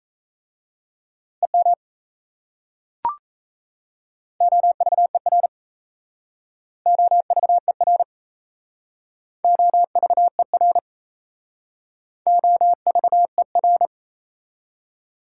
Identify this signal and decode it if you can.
Morse code, heard as a keyed tone, about 36 wpm, 700 Hz. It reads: EM OVER OVER OVER OVER